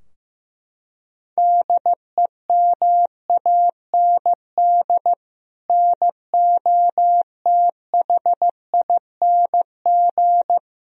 Morse code, heard as a keyed tone, about 15 wpm, 700 Hz. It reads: DEMAND NOTHING
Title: Sentences from Top 1000 Words - Review Part 4 - 15wpm